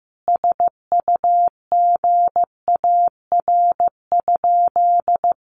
Morse code, heard as a keyed tone, about 15 wpm, 700 Hz. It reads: SUGAR?
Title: Sentences from Top 1200 Words 15wpm